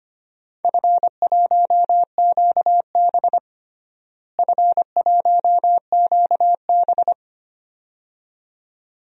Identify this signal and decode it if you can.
Morse code, heard as a keyed tone, about 25 wpm, 700 Hz. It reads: F1QB F1QB
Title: DXCC-100 Call Signs - 4 Characters - Repeated 2x - 25wpm